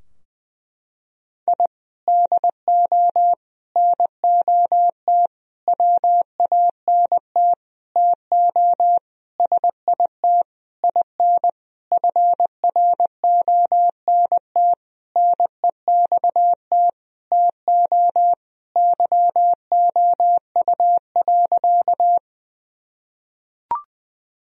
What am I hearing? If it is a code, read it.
Morse code, heard as a keyed tone, about 20 wpm, 700 Hz. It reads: I DO NOT WANT TO SIT IN FRONT NEXT TO YOU.
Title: Sentences from Top 1000 Words - Review Part 2 - 20wpm